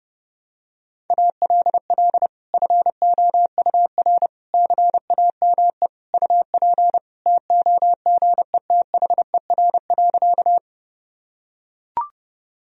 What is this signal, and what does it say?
Morse code, heard as a keyed tone, about 32 wpm, 700 Hz. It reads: ALL FOUR CAME UP TOGETHER.
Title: Sentences from Top 1000 Words - Review Part 1 - 30wpm